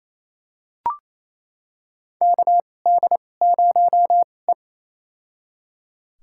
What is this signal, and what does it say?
Morse code, heard as a keyed tone, about 28 wpm, spaced as for 20 wpm, 700 Hz. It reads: KD0E
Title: USA Call Signs - 4 Characters - 2x Character Spacing - 28wpm